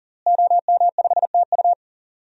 Morse code, heard as a keyed tone, about 40 wpm, 700 Hz. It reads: OM5TU
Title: DXCC-100 Call Signs - 5 Characters - Repeated 3x - 40wpm